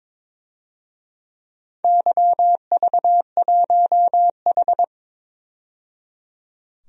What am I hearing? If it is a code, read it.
Morse code, heard as a keyed tone, about 22 wpm, 700 Hz. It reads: YV1H
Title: DXCC-100 Call Signs - 4 Characters - 22wpm